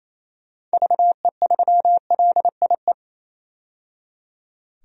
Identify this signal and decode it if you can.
Morse code, heard as a keyed tone, about 27 wpm, 700 Hz. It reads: VE3LIE
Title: DXCC-100 Call Signs - 6 Characters - 28wpm